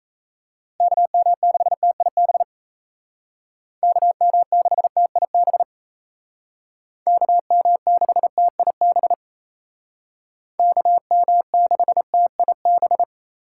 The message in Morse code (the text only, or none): KM6TIB KM6TIB KM6TIB KM6TIB